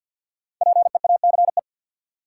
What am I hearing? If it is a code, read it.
Morse code, heard as a keyed tone, about 51 wpm, 700 Hz. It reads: PEACE
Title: Sentences from Top 1500 Words - New Word Drill Alt - 50wpm